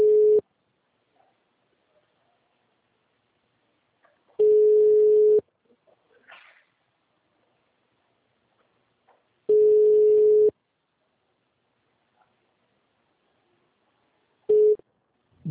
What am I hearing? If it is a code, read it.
Ringback tone (Europe, 425 Hz).